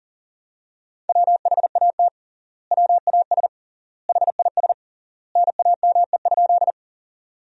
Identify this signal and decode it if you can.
Morse code, heard as a keyed tone, about 40 wpm, 700 Hz. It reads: WHAT WAS HIS NAME?
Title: 3x Word-Spacing - Sentences from Top 200 Words - 40wpm